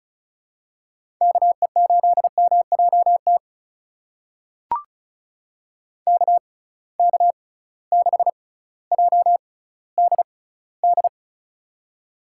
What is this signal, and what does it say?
Morse code, heard as a keyed tone, about 36 wpm, spaced as for 18 wpm, 700 Hz. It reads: KE8MJT KK6JDD
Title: USA Call Signs - 6 Characters - 6x Character Spacing - 35wpm